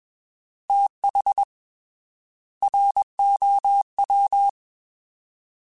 Morse code, heard as a keyed tone, about 21 wpm, 800 Hz. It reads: TH ROW
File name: TH / TH mp3s / Fast TH @20